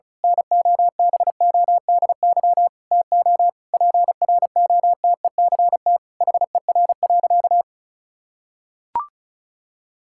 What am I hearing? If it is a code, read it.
Morse code, heard as a keyed tone, about 36 wpm, 700 Hz. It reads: NOBODY TO PROTECT HER.
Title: Sentences from Top 1600 Words 35wpm